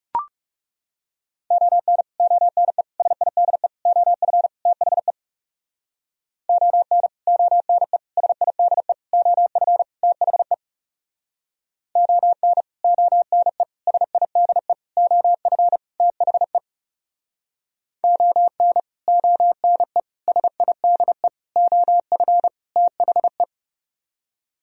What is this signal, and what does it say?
Morse code, heard as a keyed tone, about 40 wpm, 700 Hz. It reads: ON ONE SIDE OF THE ON ONE SIDE OF THE ON ONE SIDE OF THE ON ONE SIDE OF THE